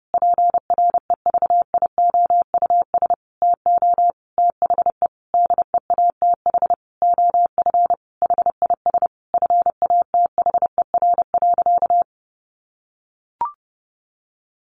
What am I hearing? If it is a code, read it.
Morse code, heard as a keyed tone, about 30 wpm, 700 Hz. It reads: PREVIOUS TO THE DEATH OF HIS FATHER.
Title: Sentences from Top 1500 Words 30wpm